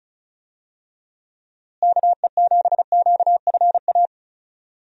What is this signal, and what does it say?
Morse code, heard as a keyed tone, about 36 wpm, 700 Hz. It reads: KE7QFA
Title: USA Call Signs - 6 Characters - Repeated 2x - 35wpm